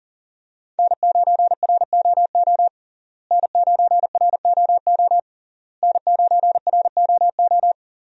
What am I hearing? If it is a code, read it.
Morse code, heard as a keyed tone, about 40 wpm, 700 Hz. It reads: N9ROO N9ROO N9ROO
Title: USA Call Signs - 5 Characters - Repeated 3x - 40wpm